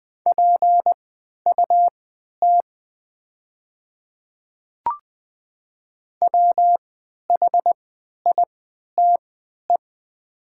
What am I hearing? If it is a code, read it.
Morse code, heard as a keyed tone, about 20 wpm, spaced as for 11 wpm, 700 Hz. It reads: PUT WHITE
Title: 3x Character Spacing - Top 500 Words - 20wpm